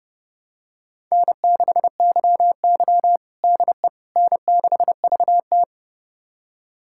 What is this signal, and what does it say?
Morse code, heard as a keyed tone, about 30 wpm, 700 Hz. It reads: N6YY DE N6VT
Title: USA Callsign DE Callsign - 4 Characters - 30wpm